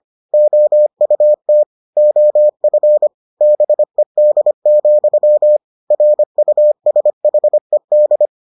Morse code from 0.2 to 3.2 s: OUT OF

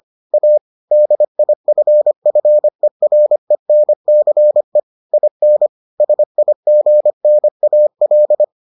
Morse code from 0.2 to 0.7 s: A

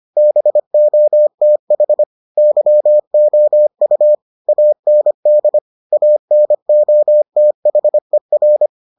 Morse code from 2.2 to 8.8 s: YOU AND ANOTHER

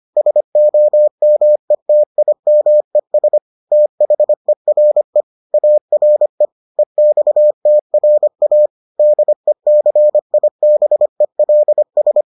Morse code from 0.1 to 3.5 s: SOMETIMES

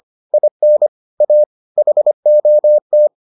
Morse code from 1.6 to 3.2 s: HOT

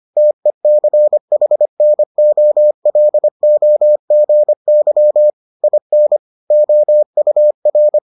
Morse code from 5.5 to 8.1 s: IN OUR

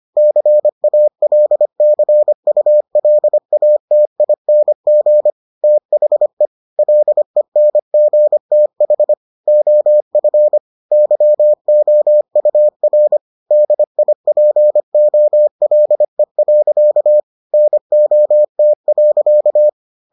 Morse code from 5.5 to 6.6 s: THE

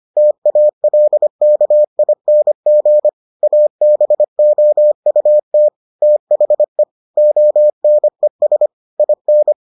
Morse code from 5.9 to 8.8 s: THE ONES

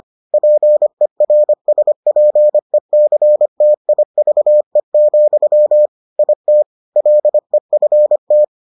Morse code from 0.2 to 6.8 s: PERSPECTIVE, IT

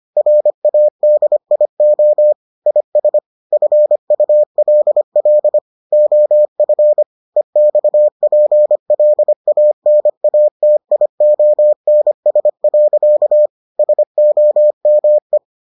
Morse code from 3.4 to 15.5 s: FULL OF EXPLANATIONS. SOME

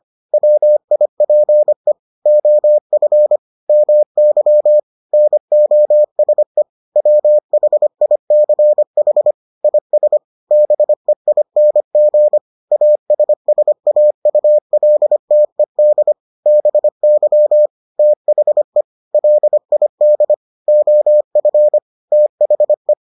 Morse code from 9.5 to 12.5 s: IS BEING

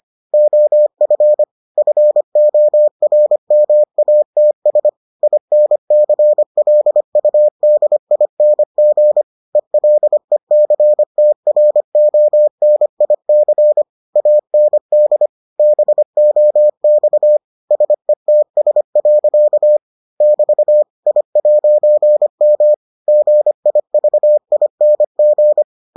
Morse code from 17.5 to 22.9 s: SETS. <BT> I'M

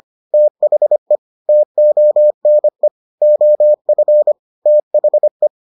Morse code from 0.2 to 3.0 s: THE TONE

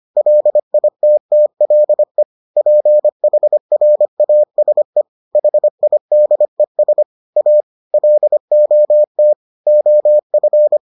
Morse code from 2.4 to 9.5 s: PHRASE HIDES A LOT

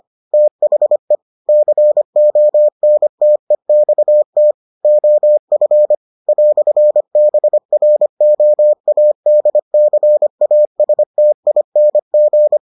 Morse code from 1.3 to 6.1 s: CONTEXT OF

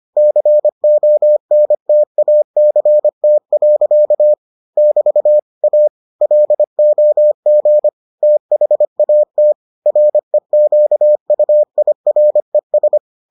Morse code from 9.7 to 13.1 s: REQUIRES

